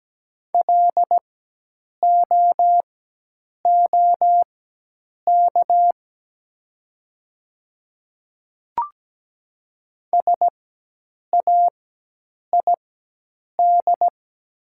Morse code, heard as a keyed tone, about 17 wpm, spaced as for 8 wpm, 690 Hz. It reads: LOOK SAID
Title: 4x Character Spacing - Top 300 Words - 17wpm